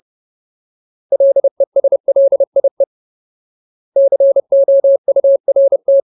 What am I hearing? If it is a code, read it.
Morse code, heard as a keyed tone, about 30 wpm, 550 Hz. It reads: LESLIE COURT
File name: Ultimate Word List / 30WPM Ultimate